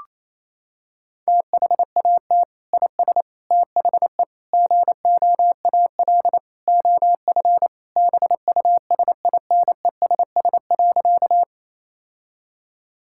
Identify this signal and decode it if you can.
Morse code, heard as a keyed tone, about 27 wpm, 700 Hz. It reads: THAT IS THE GOAL OF BUSINESS.
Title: Sentences from Top 1000 Words - Review Part 5 - 28wpm